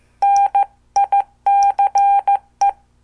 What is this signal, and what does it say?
Morse code, heard as a keyed tone, about 14 wpm, 800 Hz. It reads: NICE
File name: morse / qso